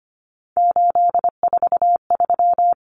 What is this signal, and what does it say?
Morse code, heard as a keyed tone, about 25 wpm, 700 Hz. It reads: 843
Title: Common Cut Numbers - 3-Digit 25wpm